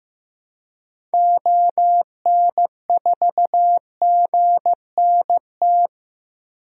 Morse code, heard as a keyed tone, about 15 wpm, 700 Hz. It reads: ON4GNT
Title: DXCC-100 Call Signs - 6 Characters - 15wpm